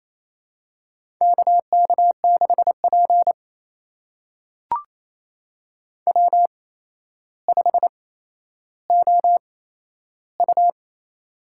Morse code, heard as a keyed tone, about 28 wpm, spaced as for 12 wpm, 700 Hz. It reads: KK6P W5OU
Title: USA Call Signs - 4 Characters - 8x Character Spacing - 28wpm